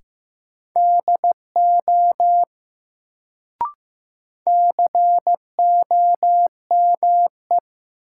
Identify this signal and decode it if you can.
Morse code, heard as a keyed tone, about 15 wpm, 700 Hz. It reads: DO COME
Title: Sets of 2 Words - New Word Drill - 15wpm